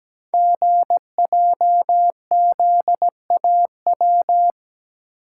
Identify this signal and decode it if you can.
Morse code, heard as a keyed tone, about 17 wpm, 700 Hz. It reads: GJZAW
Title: Code Groups - 5 Letters - 17wpm